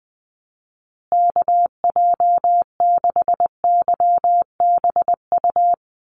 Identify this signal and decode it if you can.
Morse code, heard as a keyed tone, about 20 wpm, 700 Hz. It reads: KJ6YBU
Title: USA Call Signs Repeated 2x 20wpm